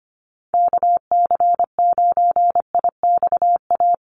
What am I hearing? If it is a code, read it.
Morse code, heard as a keyed tone, about 25 wpm, 700 Hz. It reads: KC9IXA
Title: USA Call Signs Repeated 3x 25wpm